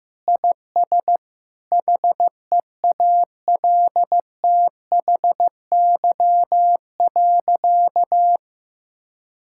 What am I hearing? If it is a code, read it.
Morse code, heard as a keyed tone, about 15 wpm, 700 Hz. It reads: IS HEALTHY.
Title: Sentences from Top 1000 Words - Review Part 5 - 15wpm